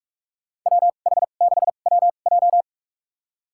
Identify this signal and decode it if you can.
Morse code, heard as a keyed tone, about 48 wpm, spaced as for 33 wpm, 700 Hz. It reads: WH6WJ
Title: USA Call Signs - 5 Characters - 2x Character Spacing - 45wpm